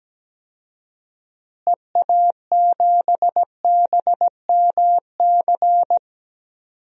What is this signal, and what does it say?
Morse code, heard as a keyed tone, about 17 wpm, 700 Hz. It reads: EA7BMC
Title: DXCC-100 Call Signs - 6 Characters - Repeated 2x - 17wpm